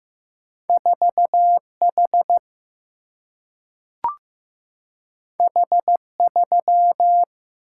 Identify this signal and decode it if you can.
Morse code, heard as a keyed tone, about 15 wpm, 700 Hz. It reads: 4 H H 3
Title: Single-Letter Single-Number 15wpm